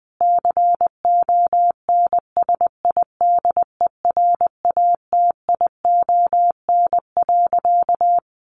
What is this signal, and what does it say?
Morse code, heard as a keyed tone, about 19 wpm, 700 Hz. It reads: CONSIDERATION.
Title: Sentences from Top 1500 Words 20wpm